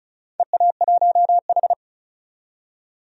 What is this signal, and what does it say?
Morse code, heard as a keyed tone, about 36 wpm, 690 Hz. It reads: EA1H
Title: DXCC-100 Call Signs - 4 Characters - 35wpm